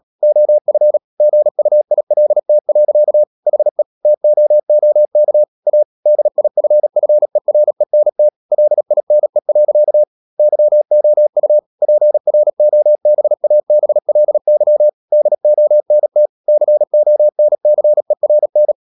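Morse code, heard as a keyed tone, about 37 wpm, 600 Hz. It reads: OF GUILT. HE TOOK A DIFFERENT LINE. YOU PROBABLY DONT CONCERN
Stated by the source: Morse code practice at 35–39 wpm sending English text of Ken Follett's novel Never, from CW med Gnister.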